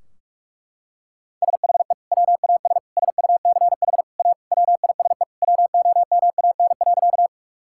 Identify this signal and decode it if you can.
Morse code, heard as a keyed tone, about 42 wpm, 700 Hz. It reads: SHE WAS SUCH A WISE WOMAN.
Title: Sentences from Top 1600 Words 45wpm